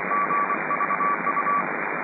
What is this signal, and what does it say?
Morse code, heard as a keyed tone, about 24 wpm, 1150 Hz, with radio noise. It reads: GVU